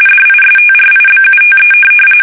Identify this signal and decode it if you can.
RTTY, about 45.45 baud. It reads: DISTRIBUTE
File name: fsk850 / teletype-corp